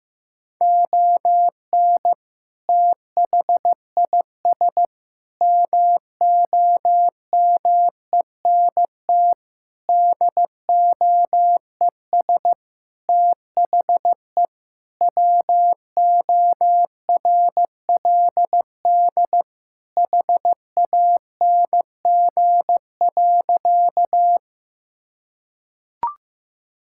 Morse code, heard as a keyed tone, about 16 wpm, 700 Hz. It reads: ON THIS MOMENT DOES THE WORLD HANG.
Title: Sentences from Top 1600 Words 15wpm